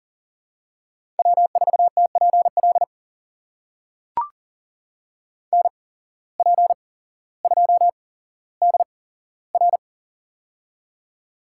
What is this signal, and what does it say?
Morse code, heard as a keyed tone, about 40 wpm, spaced as for 17 wpm, 700 Hz. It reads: W4TPL NP2DR
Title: USA Call Signs - 5 Characters - 8x Character Spacing - 40wpm